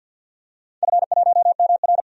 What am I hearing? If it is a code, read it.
Morse code, heard as a keyed tone, about 49 wpm, 700 Hz. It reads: F1MR
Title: DXCC-100 Call Signs - 4 Characters - 50wpm